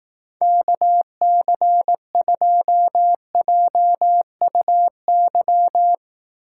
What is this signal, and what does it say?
Morse code, heard as a keyed tone, about 18 wpm, 700 Hz. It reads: KC2JUY
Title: Speed Racing - USA 2x3 Call Signs 15wpm